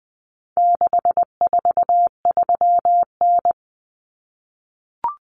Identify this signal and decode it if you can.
Morse code, heard as a keyed tone, about 20 wpm, 700 Hz. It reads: 643N
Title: Common Cut Numbers - 4-Digit 20wpm